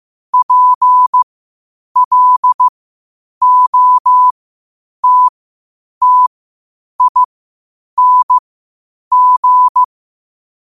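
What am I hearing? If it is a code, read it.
Morse code, heard as a keyed tone, about 15 wpm, spaced as for 8 wpm, 1000 Hz. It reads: PLOTTING
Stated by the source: Morse code quotes at 8 WPM.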